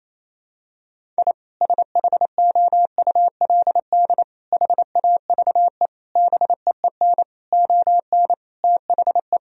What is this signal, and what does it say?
Morse code, heard as a keyed tone, about 28 wpm, 700 Hz. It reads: I SHOULD HAVE BEEN ON THE
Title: Sentences from Top 1000 Words - Review Part 1 - 28wpm